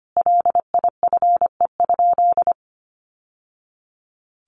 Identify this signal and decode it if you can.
Morse code, heard as a keyed tone, about 25 wpm, 700 Hz. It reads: LIFE?
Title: Sentences from Top 1100 Words 25wpm